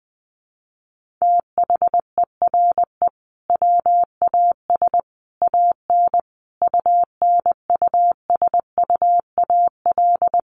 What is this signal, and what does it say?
Morse code, heard as a keyed tone, about 20 wpm, 700 Hz. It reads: THERE WAS AN UNUSUAL
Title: Sentences from Top 1400 Words 20wpm